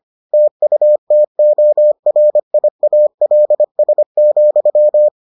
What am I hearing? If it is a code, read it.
Morse code, heard as a keyed tone, about 25 wpm, 600 Hz. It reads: TUTORIALS,